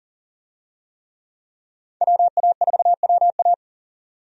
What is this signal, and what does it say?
Morse code, heard as a keyed tone, about 40 wpm, 700 Hz. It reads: WA4WA
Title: USA Call Signs - 5 Characters - 40wpm